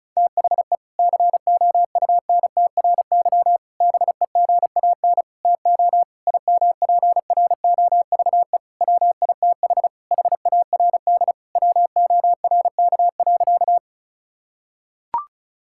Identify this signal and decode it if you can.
Morse code, heard as a keyed tone, about 36 wpm, 700 Hz. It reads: THE COUNTRY BEGAN TO IMPROVE WITH HARD WORK.
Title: Sentences from Top 1000 Words - Review Part 5 - 35wpm